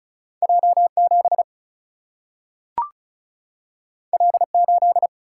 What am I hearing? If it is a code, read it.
Morse code, heard as a keyed tone, about 36 wpm, 700 Hz. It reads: J 7 L 8
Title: Single-Letter Single-Number 35wpm